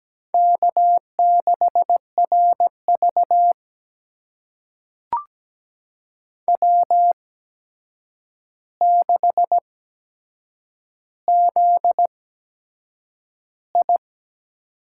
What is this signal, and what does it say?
Morse code, heard as a keyed tone, about 17 wpm, spaced as for 7 wpm, 700 Hz. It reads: K6RV W6ZI
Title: USA Call Signs - 4 Characters - 8x Character Spacing - 17wpm